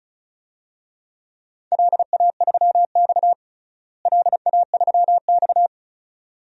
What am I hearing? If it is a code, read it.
Morse code, heard as a keyed tone, about 36 wpm, 700 Hz. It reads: LA3X LA3X
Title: DXCC-100 Call Signs - 4 Characters - Repeated 2x - 35wpm